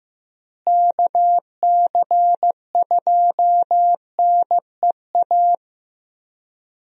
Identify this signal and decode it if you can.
Morse code, heard as a keyed tone, about 15 wpm, 700 Hz. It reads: KC2NEA
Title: USA Call Signs - 6 Characters - Repeated 2x - 15wpm